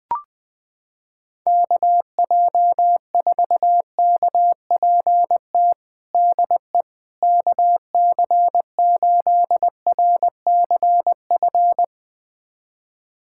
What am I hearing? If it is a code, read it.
Morse code, heard as a keyed tone, about 21 wpm, 700 Hz. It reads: KJ4KPT DE KC8RCF
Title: USA Callsign DE Callsign - 6 Characters - 20wpm